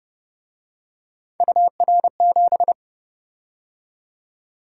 Morse code, heard as a keyed tone, about 29 wpm, 700 Hz. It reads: UR7
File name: Speed-Racing - Three Letter-Number Combinations - 25wpm